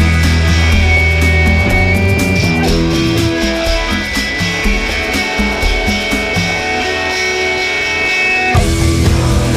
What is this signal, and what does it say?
Dial-up modem handshake.